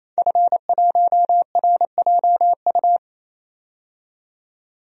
Morse code, heard as a keyed tone, about 28 wpm, 700 Hz. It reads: F1RJU
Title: DXCC-100 Call Signs - 5 Characters - 28wpm